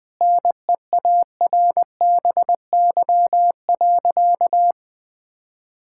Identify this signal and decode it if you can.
Morse code, heard as a keyed tone, about 20 wpm, 700 Hz. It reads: NEARBY.
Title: Sentences with 2-Word Phrases - Spoken Beginning - 20wpm